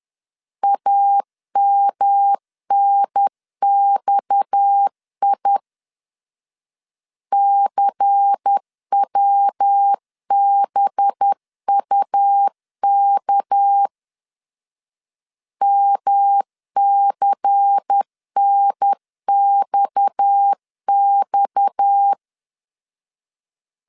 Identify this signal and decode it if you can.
Morse code, heard as a keyed tone, about 10 wpm, 800 Hz. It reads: AMNXI CWBUK MCNXX
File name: LEZIONI / 6 PKX0 / 6 PKX 10wpm 2426